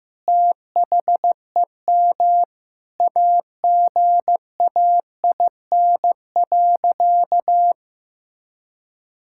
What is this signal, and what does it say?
Morse code, heard as a keyed tone, about 16 wpm, 700 Hz. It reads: THEM AGAIN.
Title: Sentences from Top 1000 Words - Review Part 6 - 15wpm